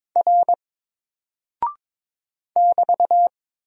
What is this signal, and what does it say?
Morse code, heard as a keyed tone, about 22 wpm, 700 Hz. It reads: R <BT>
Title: Single Character (Letter:Number:Punctuation:Prosign) Alt 22wpm